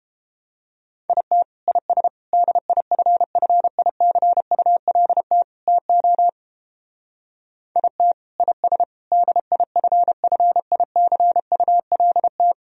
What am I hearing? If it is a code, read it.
Morse code, heard as a keyed tone, about 29 wpm, 700 Hz. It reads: IT IS DIFFICULT TO IT IS DIFFICULT